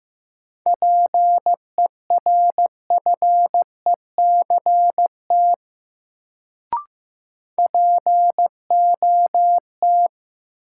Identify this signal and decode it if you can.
Morse code, heard as a keyed tone, about 15 wpm, 700 Hz. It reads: PERFECT POT
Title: Sentences from Top 1000 Words - New Word Drill - 15wpm